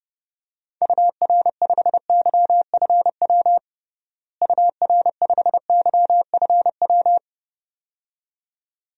Morse code, heard as a keyed tone, about 32 wpm, 700 Hz. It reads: UR5YFW UR5YFW
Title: DXCC-100 Call Signs - 6 Characters - Repeated 2x - 30wpm